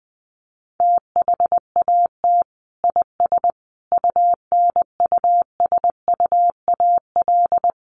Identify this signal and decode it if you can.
Morse code, heard as a keyed tone, about 20 wpm, 700 Hz. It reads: THAT IS UNUSUAL
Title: Sentences from Top 1400 Words 20wpm